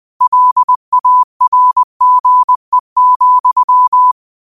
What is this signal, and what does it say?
Morse code, heard as a keyed tone, about 20 wpm, 1000 Hz. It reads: LARGE,